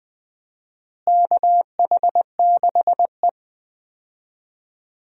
Morse code, heard as a keyed tone, about 21 wpm, 700 Hz. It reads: KH6E
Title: DXCC-100 Call Signs - 4 Characters - Repeated 2x - 20wpm